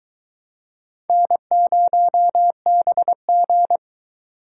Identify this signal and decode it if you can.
Morse code, heard as a keyed tone, about 23 wpm, 700 Hz. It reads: N0BG